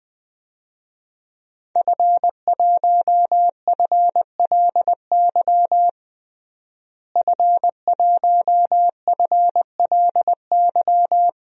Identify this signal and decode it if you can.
Morse code, heard as a keyed tone, about 20 wpm, 700 Hz. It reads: F1FLY F1FLY
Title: DXCC-100 Call Signs - 5 Characters - Repeated 3x - 20wpm